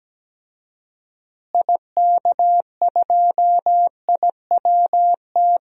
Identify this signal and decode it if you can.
Morse code, heard as a keyed tone, about 17 wpm, 700 Hz. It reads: IK2IWT
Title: DXCC-100 Call Signs - 6 Characters - Repeated 3x - 17wpm